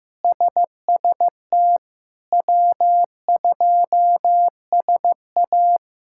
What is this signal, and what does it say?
Morse code, heard as a keyed tone, about 16 wpm, 700 Hz. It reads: SST W2SA
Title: K1USN SST QSO Practice - 15wpm